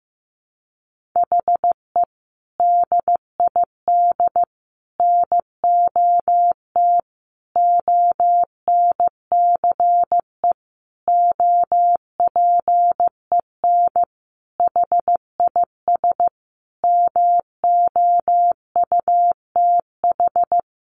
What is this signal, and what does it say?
Morse code, heard as a keyed tone, about 15 wpm, 700 Hz. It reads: HE DID NOT ONCE OPEN HIS MOUTH
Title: Sentences from Top 1500 Words 15wpm